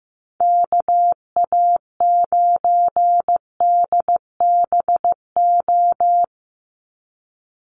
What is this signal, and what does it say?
Morse code, heard as a keyed tone, about 14 wpm, 700 Hz. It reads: KA9DBO
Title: USA Call Signs 15wpm